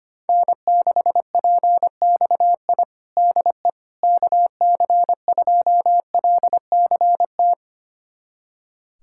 Morse code, heard as a keyed tone, about 25 wpm, 700 Hz. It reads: N6PXI DE KC2LCT
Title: USA Callsign DE Callsign 25wpm